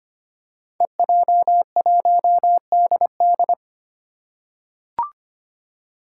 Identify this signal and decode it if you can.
Morse code, heard as a keyed tone, about 25 wpm, 700 Hz. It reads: EJ1DD
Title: Five Challenging Letter-Number Combinations 25wpm